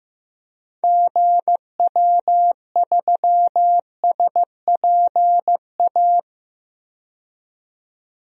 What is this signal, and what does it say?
Morse code, heard as a keyed tone, about 15 wpm, 700 Hz. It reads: GW3SPA
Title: DXCC-100 Call Signs - 6 Characters - 15wpm